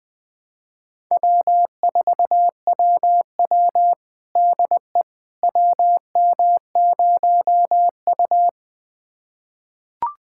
Morse code, heard as a keyed tone, about 21 wpm, 700 Hz. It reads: W4WW DE WM0U
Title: USA Callsign DE Callsign - 4 Characters - 20wpm